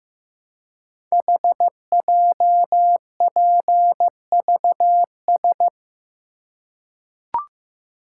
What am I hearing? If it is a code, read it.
Morse code, heard as a keyed tone, about 15 wpm, 700 Hz. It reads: HJPVS
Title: Code Groups - 5 Letters - 15wpm